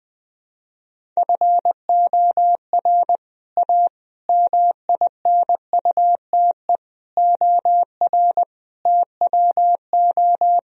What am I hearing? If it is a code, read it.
Morse code, heard as a keyed tone, about 20 wpm, 700 Hz. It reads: FOR A MINUTE OR TWO
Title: Sentences from Top 1000 Words - Review Part 1 - 20wpm